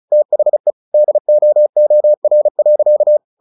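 Morse code, heard as a keyed tone, about 36 wpm, 600 Hz. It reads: THE DOOR.
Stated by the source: Morse practice at 35 wpm from CW med Gnister, sending Ken Follett's novel Never in English.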